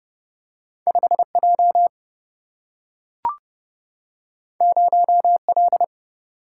Morse code, heard as a keyed tone, about 30 wpm, 700 Hz. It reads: 5J 0L
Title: Single-Letter Single-Number 30wpm